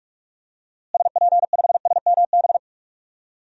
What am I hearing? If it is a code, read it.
Morse code, heard as a keyed tone, about 41 wpm, 700 Hz. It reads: SP5SMB
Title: DXCC-100 Call Signs - 6 Characters - Repeated 2x - 45wpm